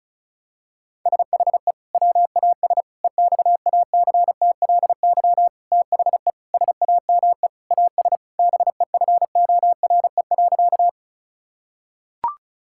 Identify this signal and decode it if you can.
Morse code, heard as a keyed tone, about 34 wpm, 700 Hz. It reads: SHE WAS EXACTLY THE SAME AS BEFORE.
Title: Sentences from Top 1000 Words - Review Part 6 - 35wpm